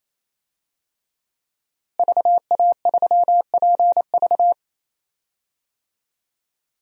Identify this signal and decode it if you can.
Morse code, heard as a keyed tone, about 27 wpm, 700 Hz. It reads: VA3PV
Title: DXCC-100 Call Signs - 5 Characters - Repeated 3x - 28wpm